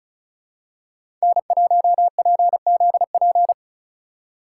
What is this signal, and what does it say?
Morse code, heard as a keyed tone, about 34 wpm, 700 Hz. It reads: N1PZP
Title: USA Call Signs - 5 Characters - 3x Character Spacing - 35wpm